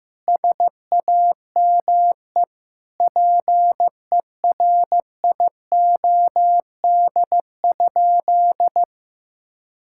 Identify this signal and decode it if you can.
Morse code, heard as a keyed tone, about 15 wpm, 700 Hz. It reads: SAME PERIOD?
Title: Sentences from Top 1000 Words - Review Part 5 - 15wpm